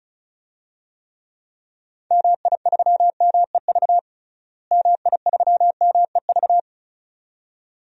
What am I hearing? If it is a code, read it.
Morse code, heard as a keyed tone, about 36 wpm, 700 Hz. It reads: MI3MEV MI3MEV
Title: DXCC-100 Call Signs - 6 Characters - Repeated 2x - 35wpm